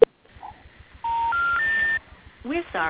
Special information tone.